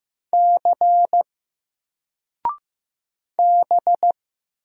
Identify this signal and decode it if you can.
Morse code, heard as a keyed tone, about 15 wpm, 700 Hz. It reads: C B
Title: Single Letter-Number 15wpm